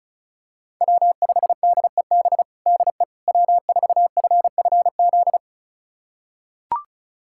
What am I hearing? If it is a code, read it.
Morse code, heard as a keyed tone, about 36 wpm, 700 Hz. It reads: W5DEB DE W4FFZ
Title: USA Callsign DE Callsign - 5 Characters - 35wpm